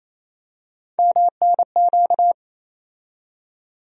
Morse code, heard as a keyed tone, about 28 wpm, 700 Hz. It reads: MNQ
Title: USA Call Sign Postfixes - 3 Letters 28wpm